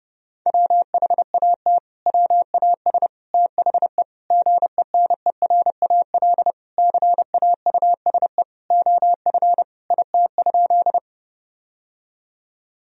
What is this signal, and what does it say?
Morse code, heard as a keyed tone, about 30 wpm, 700 Hz. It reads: WHAT WAS THE GENERAL CAUSE OF IT?
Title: Sentences from Top 1000 Words - Review Part 3 - 30wpm